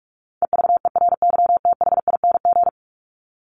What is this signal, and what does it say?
Morse code, heard as a keyed tone, about 44 wpm, 700 Hz. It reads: EVERYTHING
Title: Top 501-750 Words 45wpm